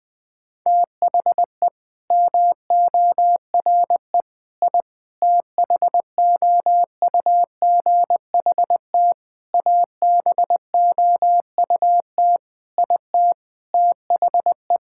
Morse code, heard as a keyed tone, about 20 wpm, 700 Hz. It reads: THE MORE I THOUGHT ABOUT IT THE
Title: Sets of 8 Words - 20wpm